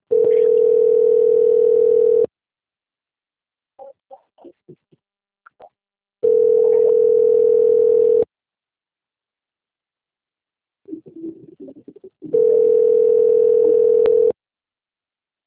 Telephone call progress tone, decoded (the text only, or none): ringback tone